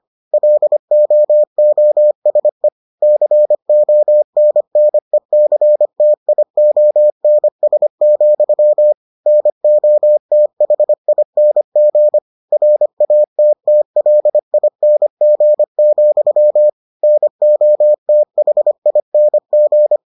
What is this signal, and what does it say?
Morse code, heard as a keyed tone, about 25 wpm, 600 Hz. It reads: LOOSE CONNECTIONS, NOTHING RATTLING, NOTHING